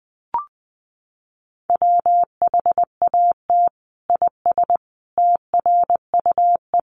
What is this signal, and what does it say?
Morse code, heard as a keyed tone, about 20 wpm, 700 Hz. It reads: WHAT IS TRUE
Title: Sentences from Top 1400 Words 20wpm